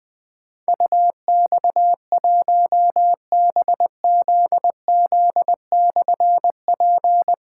Morse code, heard as a keyed tone, about 20 wpm, 700 Hz. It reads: UX1BZZ/P
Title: Difficult Call Signs 20wpm